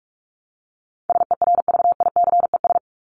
Morse code, heard as a keyed tone, about 44 wpm, 700 Hz. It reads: SERVICES
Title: ES:S Suffix - Top 100 Words with Base Words 45wpm